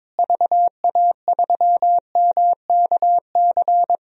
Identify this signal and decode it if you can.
Morse code, heard as a keyed tone, about 22 wpm, 700 Hz. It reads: VA3MKC
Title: DXCC-100 Call Signs - 6 Characters - Repeated 3x - 22wpm